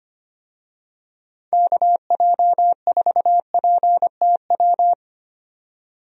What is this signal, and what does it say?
Morse code, heard as a keyed tone, about 25 wpm, 700 Hz. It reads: KJ4PTW
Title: USA Call Signs - 6 Characters - 25wpm